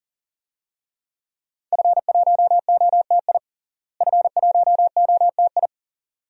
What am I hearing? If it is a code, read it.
Morse code, heard as a keyed tone, about 40 wpm, 700 Hz. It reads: F1OTI F1OTI
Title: DXCC-100 Call Signs - 5 Characters - Repeated 2x - 40wpm